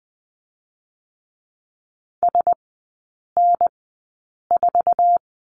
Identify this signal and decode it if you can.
Morse code, heard as a keyed tone, about 20 wpm, spaced as for 8 wpm, 690 Hz. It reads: SN4
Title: 081 Lesson 14e - Number 2 - Three Characters 20wpm